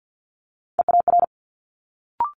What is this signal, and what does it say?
Morse code, heard as a keyed tone, about 50 wpm, 700 Hz. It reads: EAR
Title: Multi-letter Phonemes Quick Fire 50wpm